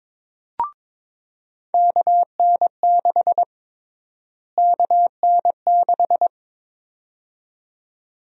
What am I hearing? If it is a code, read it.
Morse code, heard as a keyed tone, about 22 wpm, 700 Hz. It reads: KN6 KN6
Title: USA Call Sign Prefixes - Repeated 2x - 22wpm